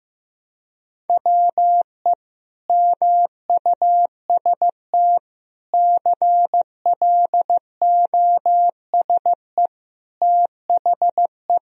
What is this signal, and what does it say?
Morse code, heard as a keyed tone, about 15 wpm, 700 Hz. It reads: WE MUST CLOSE THE
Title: Sentences from Top 1000 Words - Review Part 1 - 15wpm